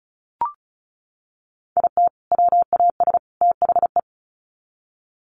Morse code, heard as a keyed tone, about 34 wpm, 700 Hz. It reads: IT WAS THE
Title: Sets of 3 Words 35wpm